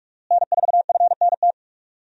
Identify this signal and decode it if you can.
Morse code, heard as a keyed tone, about 46 wpm, 700 Hz. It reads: N4FNT
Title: USA Call Signs - 5 Characters - Repeated 2x - 45wpm